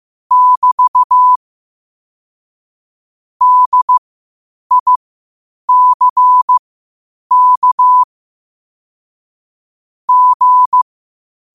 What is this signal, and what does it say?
Morse code, heard as a keyed tone, about 15 wpm, spaced as for 8 wpm, 1000 Hz. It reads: <BT> DICK G
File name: qotdfiles / 8wpm